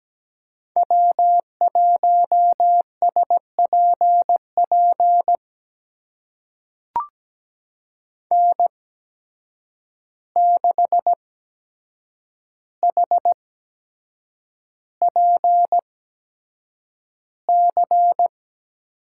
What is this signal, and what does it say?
Morse code, heard as a keyed tone, about 17 wpm, spaced as for 7 wpm, 700 Hz. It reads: W1SPP N6HPC
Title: USA Call Signs - 5 Characters - 8x Character Spacing - 17wpm